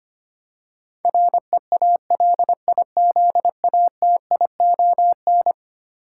Morse code, heard as a keyed tone, about 25 wpm, 700 Hz. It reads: REALIZATION
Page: morse-code-ninja-practice-25wpm